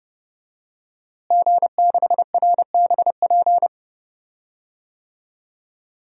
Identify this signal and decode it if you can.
Morse code, heard as a keyed tone, about 30 wpm, 700 Hz. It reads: G6RBP
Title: DXCC-100 Call Signs - 5 Characters - Repeated 2x - 30wpm